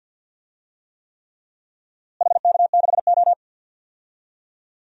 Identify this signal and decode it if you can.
Morse code, heard as a keyed tone, about 50 wpm, 700 Hz. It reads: HK6O
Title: DXCC-100 Call Signs - 4 Characters - Repeated 3x - 50wpm